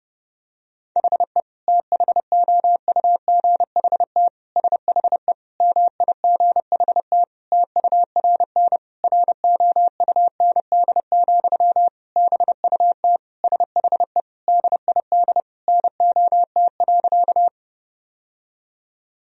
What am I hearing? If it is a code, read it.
Morse code, heard as a keyed tone, about 30 wpm, 700 Hz. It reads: HE THOUGHT SHE MIGHT TURN ROUND, BUT SHE DID NOT.
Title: Sentences from Top 1000 Words - Review Part 1 - 30wpm